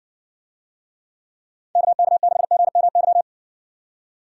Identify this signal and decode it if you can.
Morse code, heard as a keyed tone, about 55 wpm, 700 Hz. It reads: KK6GMY